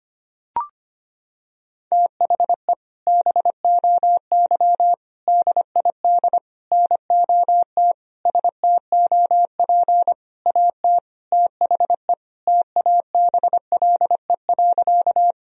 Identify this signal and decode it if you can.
Morse code, heard as a keyed tone, about 25 wpm, 700 Hz. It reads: THE BOY DID NOT STOP AT THE TABLE.
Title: Sentences from Top 1000 Words - Review Part 2 - 25wpm